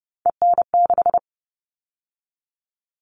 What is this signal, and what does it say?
Morse code, heard as a keyed tone, about 30 wpm, 700 Hz. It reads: EN6
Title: All Cut Numbers - 3-Digit 30wpm